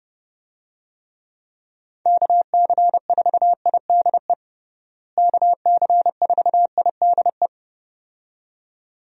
Morse code, heard as a keyed tone, about 30 wpm, 700 Hz. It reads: KC4IDE KC4IDE
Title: USA Call Signs - 6 Characters - Repeated 2x - 30wpm